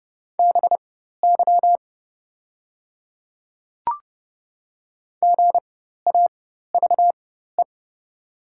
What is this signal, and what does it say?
Morse code, heard as a keyed tone, about 32 wpm, spaced as for 14 wpm, 690 Hz. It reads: BY GAVE
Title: 4x Character Spacing - Top 500 Words - 30wpm